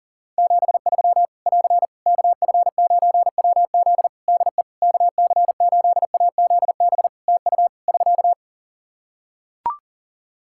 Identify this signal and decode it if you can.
Morse code, heard as a keyed tone, about 37 wpm, 700 Hz. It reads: 73 <AR> KF9WZ DE KC8AZB TU <SK>
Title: Practice QSO - Exchange 4 - Ending - 40wpm